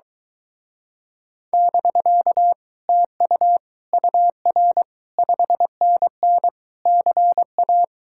Morse code, heard as a keyed tone, about 23 wpm, 700 Hz. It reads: <BK> TU UR 5NN CA